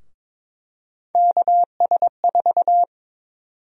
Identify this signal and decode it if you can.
Morse code, heard as a keyed tone, about 22 wpm, 700 Hz. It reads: KS4
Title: USA Call Sign Prefixes - 22wpm